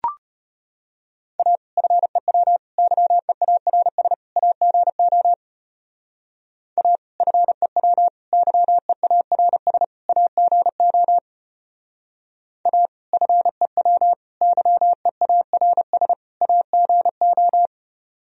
Morse code, heard as a keyed tone, about 31 wpm, 700 Hz. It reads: A FEW YEARS AGO A FEW YEARS AGO A FEW YEARS AGO
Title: Speed-Racing - Sets of 4 Words 25wpm